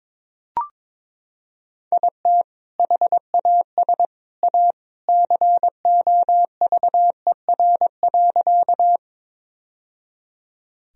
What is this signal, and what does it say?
Morse code, heard as a keyed tone, about 22 wpm, 700 Hz. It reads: IT HAS A COVER.
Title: Sentences from Top 1000 Words - Review Part 1 - 22wpm